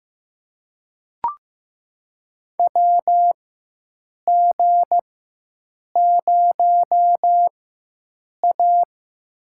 Morse code, heard as a keyed tone, about 15 wpm, spaced as for 7 wpm, 700 Hz. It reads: WG0A
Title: USA Call Signs - 4 Characters - 4x Character Spacing - 15wpm